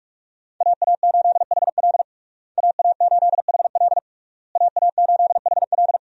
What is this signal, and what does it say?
Morse code, heard as a keyed tone, about 46 wpm, 700 Hz. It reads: AA8HL AA8HL AA8HL
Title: USA Call Signs - 5 Characters - Repeated 3x - 45wpm